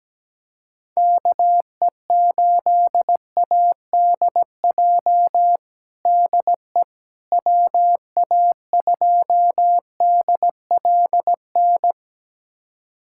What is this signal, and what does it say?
Morse code, heard as a keyed tone, about 17 wpm, 700 Hz. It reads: KE8ADJ DE WA2DLN
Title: USA Callsign DE Callsign - 6 Characters - 17wpm